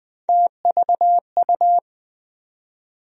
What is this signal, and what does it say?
Morse code, heard as a keyed tone, about 20 wpm, 700 Hz. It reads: TVU
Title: USA Call Sign Postfixes - 3 Letters 20wpm